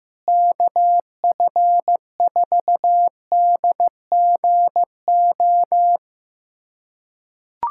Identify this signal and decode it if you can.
Morse code, heard as a keyed tone, about 15 wpm, 700 Hz. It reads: KF4DGO
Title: USA Callsign DE Callsign - 6 Characters - 15wpm